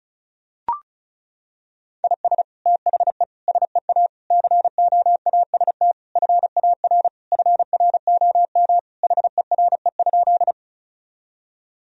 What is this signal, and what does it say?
Morse code, heard as a keyed tone, about 34 wpm, 700 Hz. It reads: IS THE SEA COAST FAR FROM HERE?
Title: Sentences from Top 1000 Words - Review Part 4 - 35wpm